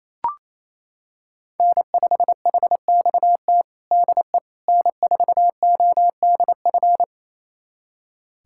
Morse code, heard as a keyed tone, about 28 wpm, 700 Hz. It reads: N5HXT DE N4ODF
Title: USA Callsign DE Callsign - 5 Characters - 28wpm